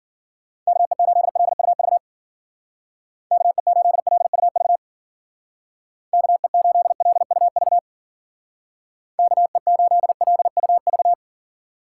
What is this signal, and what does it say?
Morse code, heard as a keyed tone, about 50 wpm, 700 Hz. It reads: KE8LUV KE8LUV KE8LUV KE8LUV
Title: Speed Racing - USA 2x3 Call Signs 40wpm